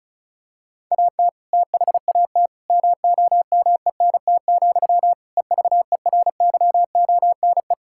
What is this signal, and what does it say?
Morse code, heard as a keyed tone, about 36 wpm, 690 Hz. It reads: AT THAT MOMENT, EVERYONE
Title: Sentences from Top 1000 Words - Review Part 5 - 35wpm